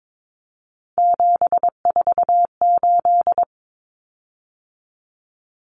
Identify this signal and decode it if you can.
Morse code, heard as a keyed tone, about 22 wpm, 700 Hz. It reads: 748
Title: Common Cut Numbers - 3-Digit 22wpm